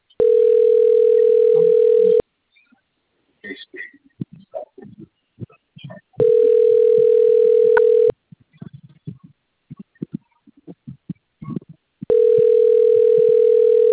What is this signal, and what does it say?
Ringback tone (North America), 440 + 480 Hz.